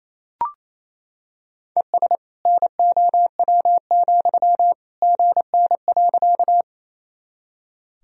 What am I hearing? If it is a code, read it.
Morse code, heard as a keyed tone, about 28 wpm, 700 Hz. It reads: ES NOW, GN.
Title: Sentences from Top 300 Words and Abbreviations 28wpm